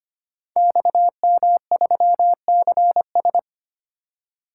Morse code, heard as a keyed tone, about 25 wpm, 700 Hz. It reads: XM3CS
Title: Code Groups - 5 Letter:Numbers - 25wpm